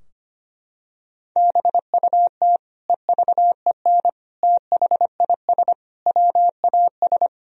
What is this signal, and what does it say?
Morse code, heard as a keyed tone, about 25 wpm, 700 Hz. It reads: BUT EVEN THIS WAS
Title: Sentences from Top 1000 Words - Review Part 2 - 25wpm